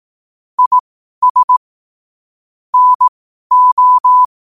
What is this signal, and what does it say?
Morse code, heard as a keyed tone, about 18 wpm, spaced as for 12 wpm, 1000 Hz. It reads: IS NO
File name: qotdfiles / 12wpm